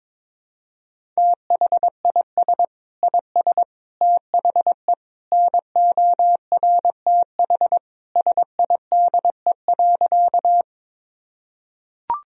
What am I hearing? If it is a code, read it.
Morse code, heard as a keyed tone, about 22 wpm, 700 Hz. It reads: THIS IS THE NORTH SIDE.
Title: Sentences from Top 1000 Words - Review Part 1 - 22wpm